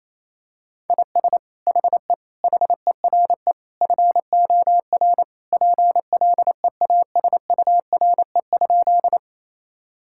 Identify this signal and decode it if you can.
Morse code, heard as a keyed tone, about 28 wpm, 700 Hz. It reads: IS HE HERE FOR PLEASURE?
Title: Sentences from Top 1600 Words 28wpm